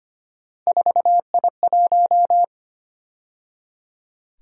Morse code, heard as a keyed tone, about 29 wpm, 700 Hz. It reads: 4I1
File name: Speed-Racing - Three Letter-Number Combinations - 25wpm